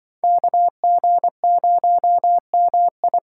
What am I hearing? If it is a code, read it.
Morse code, heard as a keyed tone, about 24 wpm, 700 Hz. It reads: KG0MI
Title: Speed Racing - USA Call Signs 20wpm